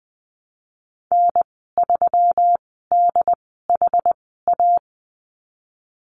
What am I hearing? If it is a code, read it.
Morse code, heard as a keyed tone, about 20 wpm, spaced as for 15 wpm, 700 Hz. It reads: N3DHA
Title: 161 Lesson 26g - Letter g - Call Signs 20wpm